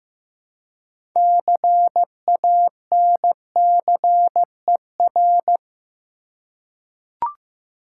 Morse code, heard as a keyed tone, about 15 wpm, 700 Hz. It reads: CANCER
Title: Six Letter Words - 15wpm